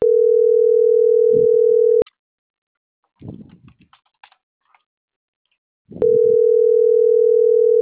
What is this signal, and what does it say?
Ringback tone (North America), 440 + 480 Hz.